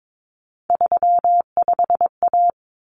Morse code, heard as a keyed tone, about 22 wpm, 700 Hz. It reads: 35A
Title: Common Cut Numbers - 3-Digit 22wpm